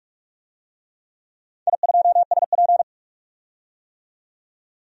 Morse code, heard as a keyed tone, about 41 wpm, 700 Hz. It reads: I2SP
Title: DXCC-100 Call Signs - 4 Characters - 45wpm